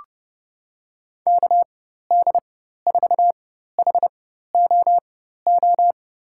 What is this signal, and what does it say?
Morse code, heard as a keyed tone, about 32 wpm, spaced as for 14 wpm, 700 Hz. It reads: KD4HOO
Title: USA Call Signs - 6 Characters - 4x Character Spacing - 30wpm